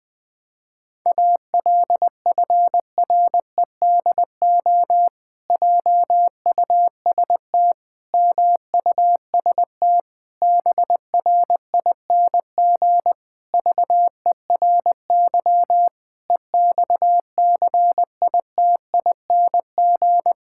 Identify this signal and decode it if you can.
Morse code, heard as a keyed tone, about 20 wpm, 700 Hz. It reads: ALFREDO JUST MUST BRING VERY EXCITING
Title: Pangram Challenge v2 - 20wpm